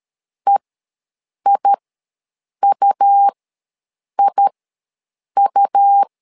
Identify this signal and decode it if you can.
Morse code, heard as a keyed tone, about 13 wpm, spaced as for 7 wpm, 800 Hz. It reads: EIUIU
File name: LEZIONI / 2 UHDN / 2 U ws 12wpm 1655